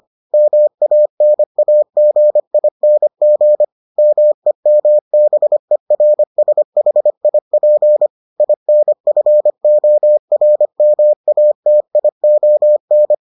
Morse code, heard as a keyed tone, about 25 wpm, 600 Hz. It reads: MANAGING MEMBERSHIP INFORMATION